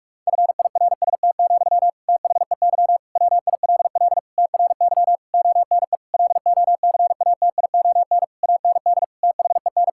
Morse code, heard as a keyed tone, about 46 wpm, 700 Hz. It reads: FIRST, THEY WILL TRY ONE LOCATION AND THEN
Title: Sentences from Top 1000 Words - Review Part 6 - 45wpm